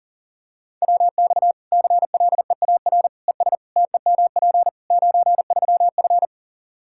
Morse code, heard as a keyed tone, about 40 wpm, 700 Hz. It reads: WX CLEAR ES TEMP 93F
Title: K1USN SST QSO Practice 40wpm